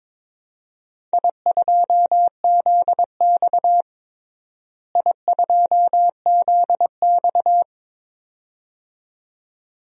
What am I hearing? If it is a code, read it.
Morse code, heard as a keyed tone, about 21 wpm, 700 Hz. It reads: I2ZX I2ZX
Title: DXCC-100 Call Signs - 4 Characters - Repeated 2x - 22wpm